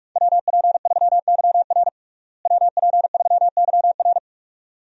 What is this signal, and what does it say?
Morse code, heard as a keyed tone, about 46 wpm, 700 Hz. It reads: WP3YR WP3YR
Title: USA Call Signs - 5 Characters - Repeated 2x - 45wpm